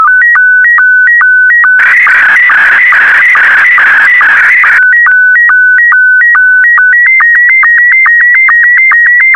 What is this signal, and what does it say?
SSTV (slow-scan television), Scottie 1.